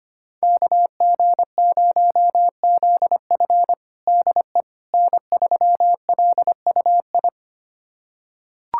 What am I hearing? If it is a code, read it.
Morse code, heard as a keyed tone, about 25 wpm, 700 Hz. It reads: KG0ZF DE N3LUI
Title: USA Callsign DE Callsign - 5 Characters - 25wpm